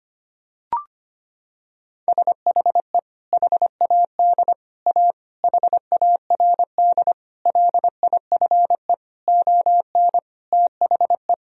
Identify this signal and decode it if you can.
Morse code, heard as a keyed tone, about 25 wpm, 700 Hz. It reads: SHE HAD A HARD LIFE ON THE
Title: Sentences from Top 1000 Words - Review Part 2 - 25wpm